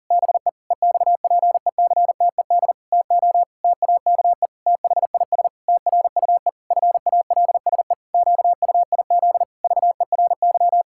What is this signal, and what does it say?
Morse code, heard as a keyed tone, about 37 wpm, 700 Hz. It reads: BE EXPECTED TO TAKE THIS TRUE FALSE QUIZ VERY
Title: Pangram Challenge v2 - 40wpm